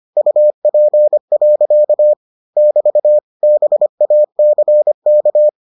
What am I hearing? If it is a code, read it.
Morse code, heard as a keyed tone, about 25 wpm, 600 Hz. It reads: UP. <BT> BACK